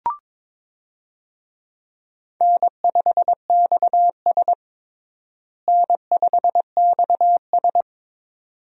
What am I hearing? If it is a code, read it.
Morse code, heard as a keyed tone, about 22 wpm, 700 Hz. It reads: N5XS N5XS
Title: USA Call Signs - 4 Characters - Repeated 2x - 22wpm